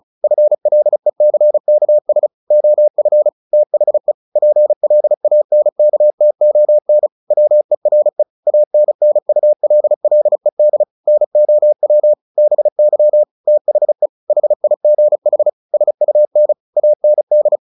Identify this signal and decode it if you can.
Morse code, heard as a keyed tone, about 34 wpm, 600 Hz. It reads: FLECKS OF THE PLANKTON WERE ANNULLED NOW BY THE HIGH SUN AND
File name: the old man and the sea / the old man and the sea 1 35wpm